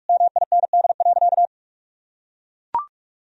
Morse code, heard as a keyed tone, about 41 wpm, 700 Hz. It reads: MIND.
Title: Sentences from Top 1000 Words - Review Part 1 - 45wpm